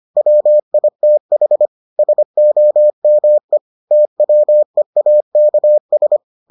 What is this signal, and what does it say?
Morse code, heard as a keyed tone, about 25 wpm, 600 Hz. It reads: WITH SOME TWEAKS